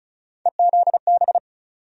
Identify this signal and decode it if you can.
Morse code, heard as a keyed tone, about 36 wpm, 700 Hz. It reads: EZB
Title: USA Call Sign Postfixes - 3 Letters 35wpm